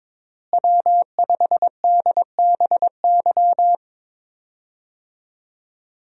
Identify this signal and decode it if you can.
Morse code, heard as a keyed tone, about 22 wpm, 700 Hz. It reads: W5DBY
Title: USA Call Signs - 5 Characters - Repeated 2x - 22wpm